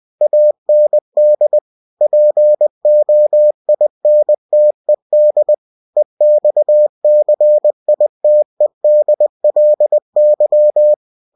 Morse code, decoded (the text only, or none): AND POINTED EXCITEDLY